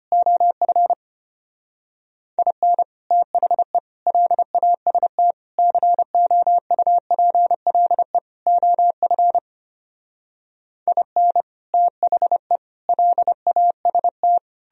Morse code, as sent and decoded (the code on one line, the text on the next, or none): --- ..-. / .. -. / - .... / .-.. .- ... - / -.-. --- ..- .--. .-.. / --- ..-. / .. -. / - .... / .-.. .- ... -
OF IN THE LAST COUPLE OF IN THE LAST